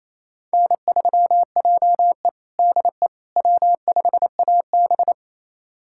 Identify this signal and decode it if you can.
Morse code, heard as a keyed tone, about 28 wpm, 700 Hz. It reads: N3JE DE W5AB
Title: USA Callsign DE Callsign - 4 Characters - 28wpm